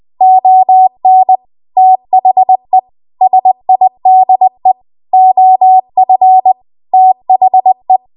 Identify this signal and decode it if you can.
Morse code, heard as a keyed tone, about 20 wpm, 750 Hz. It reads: ON THE SIDE OF THE
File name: Morse / 20WPM